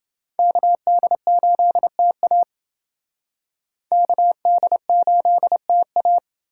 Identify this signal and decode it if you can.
Morse code, heard as a keyed tone, about 29 wpm, 700 Hz. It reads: KD8TA KD8TA